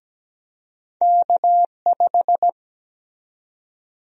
Morse code, heard as a keyed tone, about 17 wpm, 700 Hz. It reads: K5
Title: USA Call Sign Prefixes - Repeated 2x - 17wpm